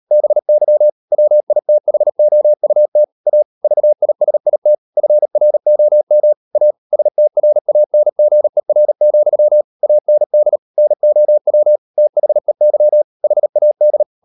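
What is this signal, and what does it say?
Morse code, heard as a keyed tone, about 38 wpm, 600 Hz. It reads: BY WITHOUT A VISIT FROM A STRANGER, AND NOW THEY HAD
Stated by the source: Morse code practice at 35–39 wpm sending English text of Ken Follett's novel Never, from CW med Gnister.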